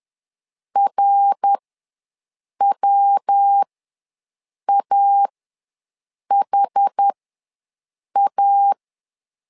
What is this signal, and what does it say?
Morse code, heard as a keyed tone, about 10 wpm, spaced as for 6 wpm, 800 Hz. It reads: RWAHA
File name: LEZIONI / 3 MWRO / 3 MWR ws 10wpm 2151